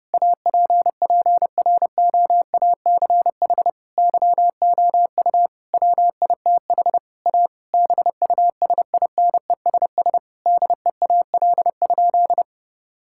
Morse code, heard as a keyed tone, about 32 wpm, 700 Hz. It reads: APPROACH YOU WITH A BUSINESS DEAL?
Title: Sentences from Top 1000 Words - Review Part 6 - 30wpm